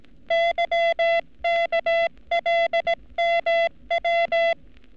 Morse code, heard as a keyed tone, about 18 wpm, 680 Hz. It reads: YKLMW
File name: training / NAVPERS 11224-11235 / NAVPERS 11233 RA RB